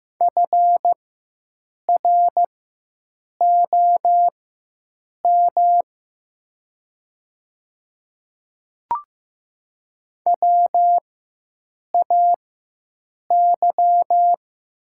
Morse code, heard as a keyed tone, about 15 wpm, spaced as for 7 wpm, 700 Hz. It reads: FROM WAY
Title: 4x Character Spacing - Top 100 Words - 15wpm